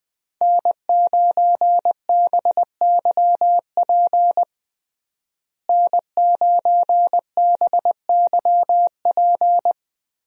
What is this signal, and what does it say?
Morse code, heard as a keyed tone, about 20 wpm, 700 Hz. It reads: N9BYP N9BYP
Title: USA Call Signs - 5 Characters - Repeated 2x - 20wpm